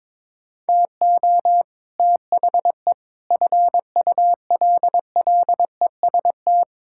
Morse code, heard as a keyed tone, about 22 wpm, 700 Hz. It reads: TO THE FULLEST
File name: practice / 22wpm / Sets of 7 Words - 22wpm